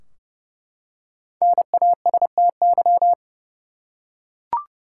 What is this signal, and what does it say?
Morse code, heard as a keyed tone, about 30 wpm, 700 Hz. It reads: NASTY
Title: Five Letter Words - 30wpm